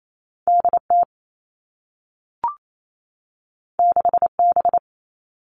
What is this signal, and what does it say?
Morse code, heard as a keyed tone, about 28 wpm, 700 Hz. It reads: DT 6B
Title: All Cut Numbers - 2-Digit 28wpm